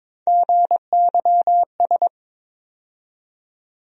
Morse code, heard as a keyed tone, about 22 wpm, 700 Hz. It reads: GYS
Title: USA Call Sign Postfixes - 3 Letters 22wpm